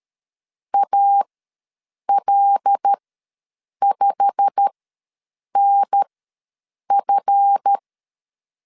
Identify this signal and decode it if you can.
Morse code, heard as a keyed tone, about 13 wpm, spaced as for 7 wpm, 800 Hz. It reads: AL5NF
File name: LEZIONI / 8 YJ45 / 8 YJ45 ws 12wpm 3146